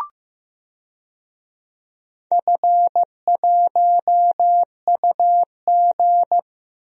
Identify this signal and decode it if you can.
Morse code, heard as a keyed tone, about 15 wpm, 700 Hz. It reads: F1UG
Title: DXCC-100 Call Signs - 4 Characters - Repeated 3x - 15wpm